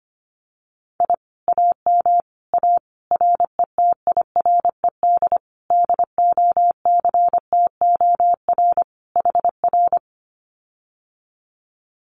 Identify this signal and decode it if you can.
Morse code, heard as a keyed tone, about 25 wpm, 700 Hz. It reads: I AM A RETIRED DOCTOR HR
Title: Common QSO Phrases Encore 25wpm